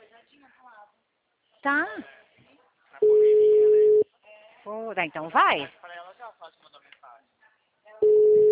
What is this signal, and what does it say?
Ringback tone (Europe, 425 Hz), with speech.